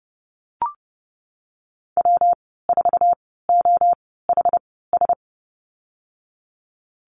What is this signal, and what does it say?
Morse code, heard as a keyed tone, about 30 wpm, spaced as for 17 wpm, 700 Hz. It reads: W4OHS
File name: Lesson 19g - Letter w - Call Signs 30wpm